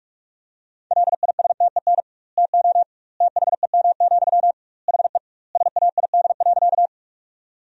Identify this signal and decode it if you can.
Morse code, heard as a keyed tone, about 42 wpm, 700 Hz. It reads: LISTEN TO THEM, HE SAID.
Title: 2x Word-Spacing - Sentences from Top 400 Words - 45wpm